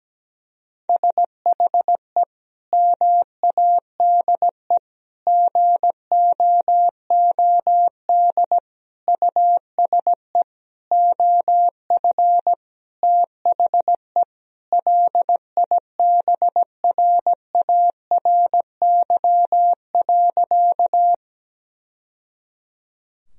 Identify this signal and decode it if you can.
Morse code, heard as a keyed tone, about 17 wpm, 700 Hz. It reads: SHE MADE GOOD USE OF THE LIBRARY.
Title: Sentences from Top 1000 Words - Review Part 4 - 17wpm